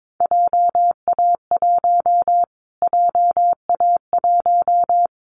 Morse code, heard as a keyed tone, about 22 wpm, 700 Hz. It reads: JA1 JA1
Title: Callsign Prefixes for Top 25 DXCC - Repeated 3x - 22wpm